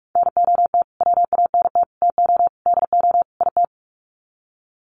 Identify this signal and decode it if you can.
Morse code, heard as a keyed tone, about 46 wpm, 700 Hz. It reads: NOT WANT TO DO IT
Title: Sets of 5 Words Encore 45wpm